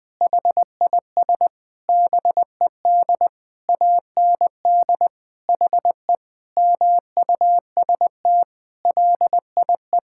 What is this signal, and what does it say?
Morse code, heard as a keyed tone, about 20 wpm, 700 Hz. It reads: HIS BED AND HE MUST LIE
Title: Sentences from Top 1600 Words 20wpm